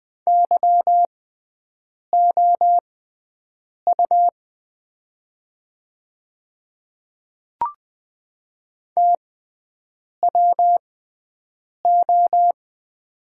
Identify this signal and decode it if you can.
Morse code, heard as a keyed tone, about 20 wpm, spaced as for 7 wpm, 690 Hz. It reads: YOU TWO
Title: 6x Character Spacing - Top 100 Words - 20wpm